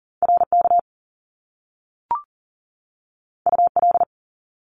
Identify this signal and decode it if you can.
Morse code, heard as a keyed tone, about 41 wpm, 700 Hz. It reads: RK UL
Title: Top Two Letter Combinations in Words 40wpm